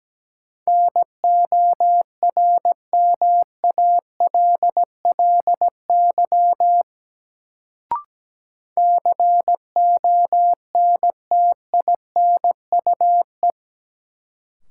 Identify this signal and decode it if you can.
Morse code, heard as a keyed tone, about 17 wpm, 700 Hz. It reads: NORMALLY CONTINUE
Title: Sentences from Top 1000 Words - Review Part 6 - New Word Drill - 17wpm